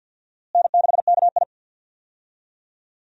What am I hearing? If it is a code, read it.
Morse code, heard as a keyed tone, about 52 wpm, 700 Hz. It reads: N6KI
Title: USA Call Signs - 4 Characters - Repeated 3x - 50wpm